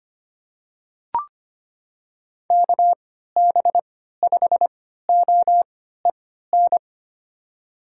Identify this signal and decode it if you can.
Morse code, heard as a keyed tone, about 25 wpm, spaced as for 14 wpm, 700 Hz. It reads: KB5OEN
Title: USA Call Signs - 6 Characters - 3x Character Spacing - 25wpm